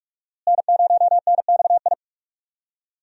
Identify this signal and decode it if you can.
Morse code, heard as a keyed tone, about 46 wpm, 700 Hz. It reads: N0NXI
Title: USA Call Signs - 5 Characters - Repeated 3x - 45wpm